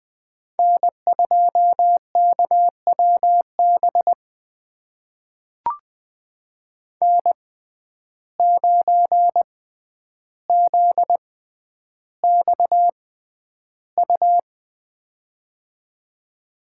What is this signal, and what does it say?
Morse code, heard as a keyed tone, about 20 wpm, spaced as for 10 wpm, 700 Hz. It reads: N2KWB N9ZXU
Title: USA Call Signs - 5 Characters - 6x Character Spacing - 20wpm